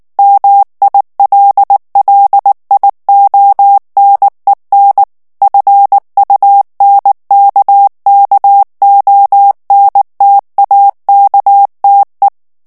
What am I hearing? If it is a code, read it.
Morse code, heard as a keyed tone, about 19 wpm, 800 Hz. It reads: MILLIONEN FUNKKONTAKTE
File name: morse mp3 / ALLEMAND texte clair 2 vitesse 18 mots minute